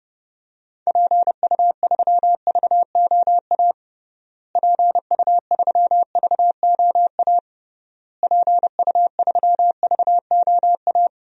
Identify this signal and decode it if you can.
Morse code, heard as a keyed tone, about 30 wpm, 700 Hz. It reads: PU3VOA PU3VOA PU3VOA
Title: DXCC-100 Call Signs - 6 Characters - Repeated 3x - 30wpm